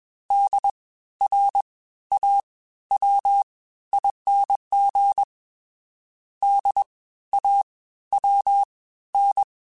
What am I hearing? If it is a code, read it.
Morse code, heard as a keyed tone, about 21 wpm, 800 Hz. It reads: D R A W ING D A W N